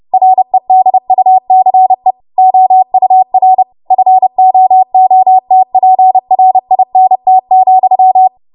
Morse code, heard as a keyed tone, about 30 wpm, 750 Hz. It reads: REDUCE OUR FOOTPRINT,
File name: Morse / 30WPM